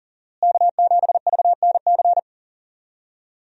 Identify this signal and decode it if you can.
Morse code, heard as a keyed tone, about 39 wpm, 700 Hz. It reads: K7VNC
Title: USA Call Signs - 5 Characters - Repeated 3x - 40wpm